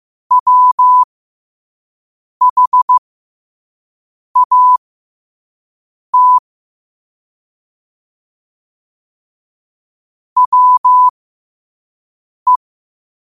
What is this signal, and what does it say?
Morse code, heard as a keyed tone, about 15 wpm, spaced as for 5 wpm, 1000 Hz. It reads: WHAT WE
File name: qotdfiles / 5wpm